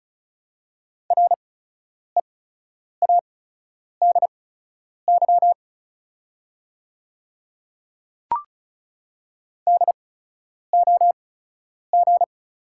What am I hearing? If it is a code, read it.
Morse code, heard as a keyed tone, about 36 wpm, spaced as for 10 wpm, 700 Hz. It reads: READY DOG
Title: 8x Character Spacing - Top 500 Words - 35wpm